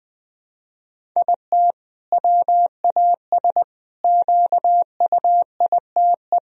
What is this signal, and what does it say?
Morse code, heard as a keyed tone, about 20 wpm, 700 Hz. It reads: IT WAS QUITE
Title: Sentences from Top 1000 Words - Review Part 1 - 20wpm